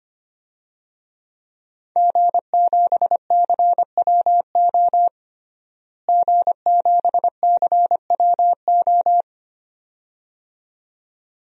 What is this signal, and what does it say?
Morse code, heard as a keyed tone, about 25 wpm, 700 Hz. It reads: G7CWO G7CWO
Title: DXCC-100 Call Signs - 5 Characters - Repeated 2x - 25wpm